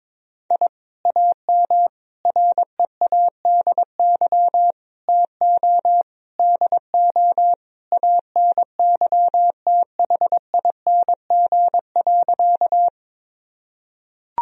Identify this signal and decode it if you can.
Morse code, heard as a keyed tone, about 22 wpm, 700 Hz. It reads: I AM READY TO DO ANYTHING.
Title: Sentences from Top 1000 Words - Review Part 2 - 22wpm